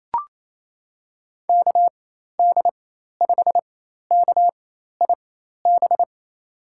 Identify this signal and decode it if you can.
Morse code, heard as a keyed tone, about 27 wpm, spaced as for 13 wpm, 700 Hz. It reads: KD5KIB
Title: USA Call Signs - 6 Characters - 4x Character Spacing - 28wpm